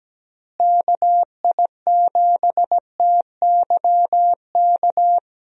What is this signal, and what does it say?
Morse code, heard as a keyed tone, about 17 wpm, 700 Hz. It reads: KI7TYK
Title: USA Call Signs - 6 Characters - Repeated 2x - 17wpm